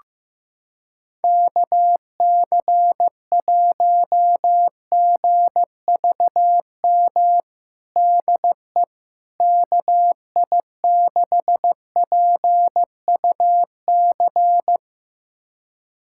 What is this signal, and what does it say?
Morse code, heard as a keyed tone, about 16 wpm, 700 Hz. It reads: KC1GVM DE KI6PUC
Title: USA Callsign DE Callsign 15wpm